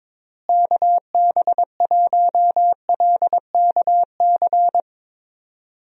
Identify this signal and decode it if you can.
Morse code, heard as a keyed tone, about 22 wpm, 700 Hz. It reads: KB1LKC
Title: USA Call Signs - 6 Characters - Repeated 2x - 22wpm